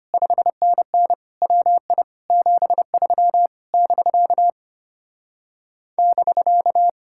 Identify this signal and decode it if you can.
Morse code, heard as a keyed tone, about 33 wpm, 700 Hz. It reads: 5NN WI 73 <BK> <BK>